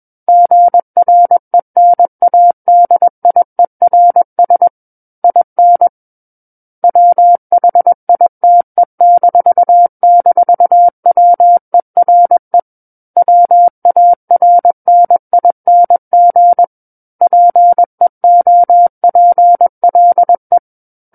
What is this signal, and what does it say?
Morse code, heard as a keyed tone, about 21 wpm, 700 Hz. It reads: GRENADIERS IN WHITE--WERE WARNING PEOPLE
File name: Book One / 12-Chapter 12 - 21 WPM